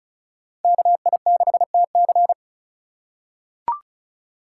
Morse code, heard as a keyed tone, about 36 wpm, 700 Hz. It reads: KI6TC
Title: USA Call Signs - 5 Characters - 4x Character Spacing - 35wpm